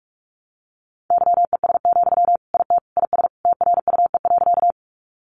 Morse code, heard as a keyed tone, about 44 wpm, 700 Hz. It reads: YES, IT IS TRUE.
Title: Sentences from Top 500 Words 45wpm